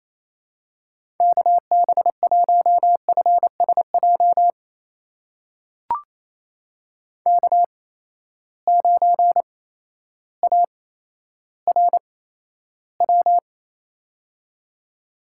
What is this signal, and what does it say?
Morse code, heard as a keyed tone, about 28 wpm, spaced as for 13 wpm, 700 Hz. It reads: KB1FSJ K9ARW
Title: USA Call Signs - 8x Character Spacing - 28wpm